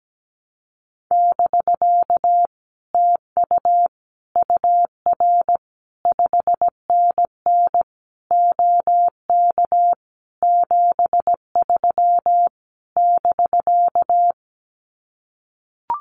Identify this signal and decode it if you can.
Morse code, heard as a keyed tone, about 17 wpm, 700 Hz. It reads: <BK> TU UR 5NN OK 73 <BK>
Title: Common QSO Phrases Encore 17wpm